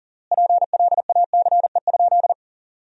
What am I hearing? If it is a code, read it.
Morse code, heard as a keyed tone, about 40 wpm, 700 Hz. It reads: PLACE?
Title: 3x Word-Spacing - Sentences from Top 200 Words - 40wpm